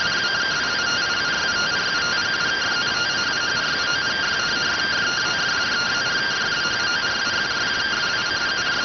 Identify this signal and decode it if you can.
Two-tone teleprinter signal, about 170 Hz shift.